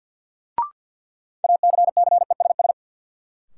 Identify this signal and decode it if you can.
Morse code, heard as a keyed tone, about 51 wpm, 700 Hz. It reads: ACCESS
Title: Sentences from Top 1000 Words - Review Part 4 - New Word Drill - 50wpm